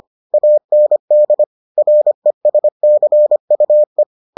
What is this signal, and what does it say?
Morse code, heard as a keyed tone, about 25 wpm, 600 Hz. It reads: AND RESCUE